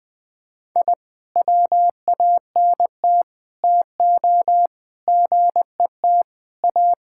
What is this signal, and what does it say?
Morse code, heard as a keyed tone, about 20 wpm, 700 Hz. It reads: I WANT TO GET A
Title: Sentences with 7-Word Phrases - 20wpm